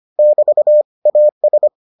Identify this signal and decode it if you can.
Morse code, heard as a keyed tone, about 25 wpm, 600 Hz. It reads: <BT> AS